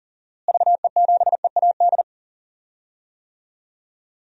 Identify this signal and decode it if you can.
Morse code, heard as a keyed tone, about 40 wpm, 700 Hz. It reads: VE7EAD
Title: DXCC-100 Call Signs - 6 Characters - Repeated 3x - 40wpm